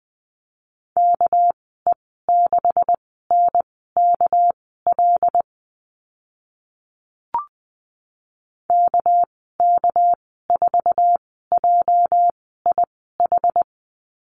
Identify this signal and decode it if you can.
Morse code, heard as a keyed tone, about 20 wpm, spaced as for 15 wpm, 700 Hz. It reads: KE6NKL KK4JIH
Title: 207 Lesson 33g - Letter j - Call Signs 20wpm